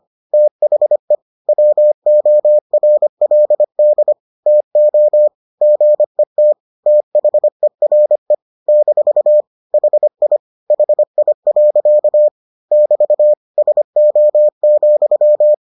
Morse code, heard as a keyed tone, about 25 wpm, 600 Hz. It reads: THE WORLD TO GET THERE - HI HI. <BT> SO,